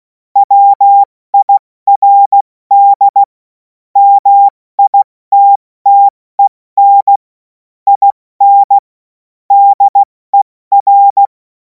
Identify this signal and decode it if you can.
Morse code, heard as a keyed tone, about 16 wpm, 800 Hz. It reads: WIRD MITTEN IN DER
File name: Morsekurs / Witze / Witz 07. Juli 2026 70 F80